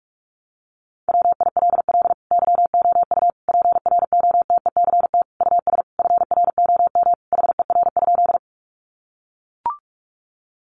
Morse code, heard as a keyed tone, about 44 wpm, 700 Hz. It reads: WILL YOU PROTECT US FROM HER?
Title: Sentences from Top 900 Words 45wpm